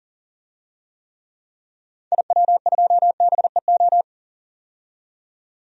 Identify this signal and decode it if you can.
Morse code, heard as a keyed tone, about 40 wpm, 700 Hz. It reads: IW2BEO
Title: DXCC-100 Call Signs - 6 Characters - 40wpm